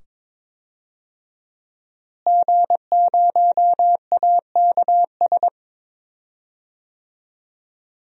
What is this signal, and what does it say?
Morse code, heard as a keyed tone, about 21 wpm, 700 Hz. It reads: G0AKS
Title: DXCC-100 Call Signs - 5 Characters - Repeated 2x - 22wpm